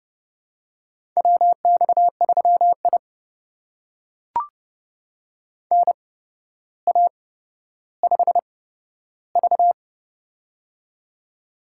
Morse code, heard as a keyed tone, about 30 wpm, spaced as for 13 wpm, 700 Hz. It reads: WX3I NA5V
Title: USA Call Signs - 4 Characters - 8x Character Spacing - 30wpm